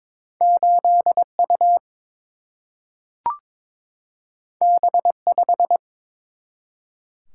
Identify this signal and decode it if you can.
Morse code, heard as a keyed tone, about 22 wpm, 700 Hz. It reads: 8U B5
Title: Single-Letter Single-Number 22wpm